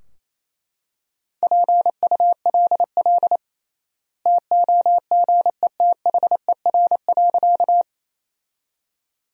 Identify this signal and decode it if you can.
Morse code, heard as a keyed tone, about 28 wpm, 700 Hz. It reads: PULL TOGETHER.
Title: 3x Word-Spacing - Sentences from Top 400 Words - 28wpm